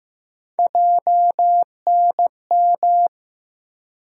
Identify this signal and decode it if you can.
Morse code, heard as a keyed tone, about 15 wpm, 700 Hz. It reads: JNM
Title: USA Call Sign Postfixes - 3 Letters 15wpm